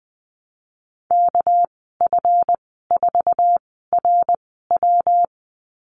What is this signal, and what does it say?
Morse code, heard as a keyed tone, about 20 wpm, spaced as for 15 wpm, 700 Hz. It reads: KF4RW
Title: 213 Lesson 34g - Number 8 - Call Signs 20wpm